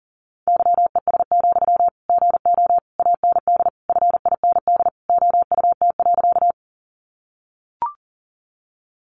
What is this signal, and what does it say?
Morse code, heard as a keyed tone, about 40 wpm, 700 Hz. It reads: YES, GO AND FIND OUT.
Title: Sentences from Top 500 Words 40wpm